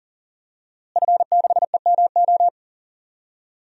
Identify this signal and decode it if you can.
Morse code, heard as a keyed tone, about 40 wpm, 700 Hz. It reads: F6EMO
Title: DXCC-100 Call Signs - 5 Characters - Repeated 3x - 40wpm